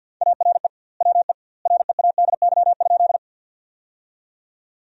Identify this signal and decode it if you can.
Morse code, heard as a keyed tone, about 50 wpm, 700 Hz. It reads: ARE WE READY?
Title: 2x Word-Spacing - Sentences from Top 400 Words - 50wpm